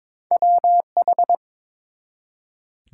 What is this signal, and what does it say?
Morse code, heard as a keyed tone, about 22 wpm, 700 Hz. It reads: WH
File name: practice / 22wpm / Two Letter-Number Combinations - 22wpm